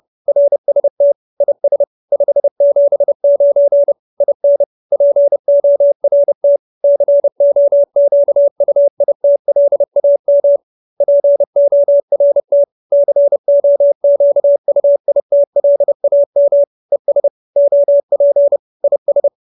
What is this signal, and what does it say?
Morse code, heard as a keyed tone, about 30 wpm, 570 Hz. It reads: RST IS 579 IN PORT COQUITLAM PORT COQUITLAM ES OP IS
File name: QSO314 30